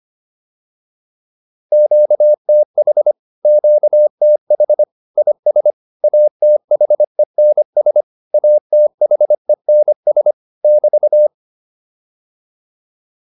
Morse code, heard as a keyed tone, about 25 wpm, 600 Hz. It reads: QTH QTH IS ATHENS ATHENS <BT>
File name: QSO408a 25